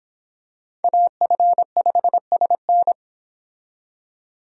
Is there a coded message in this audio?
Yes: Morse code from 0.8 to 2.9 s, heard as a keyed tone, about 25 wpm, 700 Hz. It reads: AF5SN